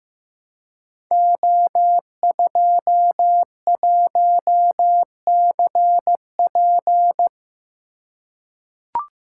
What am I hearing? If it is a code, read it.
Morse code, heard as a keyed tone, about 15 wpm, 700 Hz. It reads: O21CP
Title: Code Groups - 5 Letter:Numbers - 15wpm